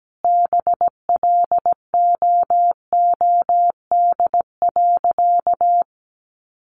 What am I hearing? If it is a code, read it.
Morse code, heard as a keyed tone, about 17 wpm, 700 Hz. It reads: BLOOD.
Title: Sentences from Top 1400 Words 17wpm